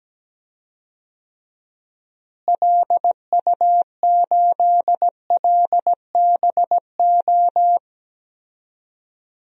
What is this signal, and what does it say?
Morse code, heard as a keyed tone, about 17 wpm, 700 Hz. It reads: LU8LBO
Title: DXCC-100 Call Signs - 6 Characters - Repeated 2x - 17wpm